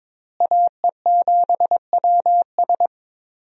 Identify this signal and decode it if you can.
Morse code, heard as a keyed tone, about 22 wpm, 700 Hz. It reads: AE7WS
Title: USA Call Signs - 5 Characters - Repeated 2x - 22wpm